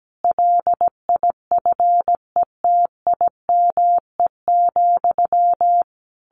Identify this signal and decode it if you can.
Morse code, heard as a keyed tone, about 17 wpm, 700 Hz. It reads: LIFETIME,
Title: A Sound Of Thunder 17wpm